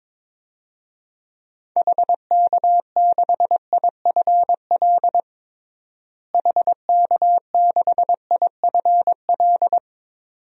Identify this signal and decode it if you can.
Morse code, heard as a keyed tone, about 22 wpm, 700 Hz. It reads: HK6IFL HK6IFL
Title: DXCC-100 Call Signs - 6 Characters - Repeated 2x - 22wpm